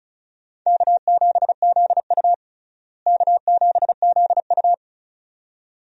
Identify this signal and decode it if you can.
Morse code, heard as a keyed tone, about 34 wpm, 700 Hz. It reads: K7ZU K7ZU
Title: USA Call Signs - 4 Characters - Repeated 2x - 35wpm